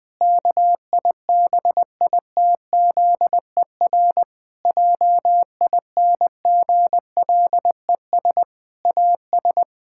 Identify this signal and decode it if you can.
Morse code, heard as a keyed tone, about 20 wpm, 700 Hz. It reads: KIBITZER JINGLES AS
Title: Pangram Challenge v2 - 20wpm